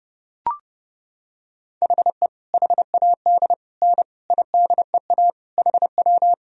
Morse code, heard as a keyed tone, about 30 wpm, 700 Hz. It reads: HE HAD N IDEA HW
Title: Sentences from Top 300 Words and Abbreviations 30wpm